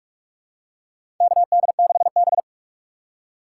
Morse code, heard as a keyed tone, about 41 wpm, 700 Hz. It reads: KD6B
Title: USA Call Signs - 3x Character Spacing - 45wpm